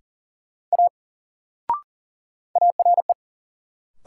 Morse code, heard as a keyed tone, about 40 wpm, 700 Hz. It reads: A ARE
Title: Sentences from Top 100 Words - New Word Drill - 40wpm